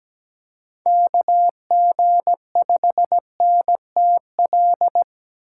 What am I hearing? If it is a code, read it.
Morse code, heard as a keyed tone, about 17 wpm, 700 Hz. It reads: KG5NTL
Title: USA Call Signs - 6 Characters - 3x Character Spacing - 17wpm